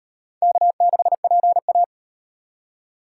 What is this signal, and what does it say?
Morse code, heard as a keyed tone, about 36 wpm, 700 Hz. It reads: K6PA